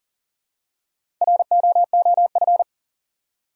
Morse code, heard as a keyed tone, about 40 wpm, 700 Hz. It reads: ROOF
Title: Four Letter Words - 40wpm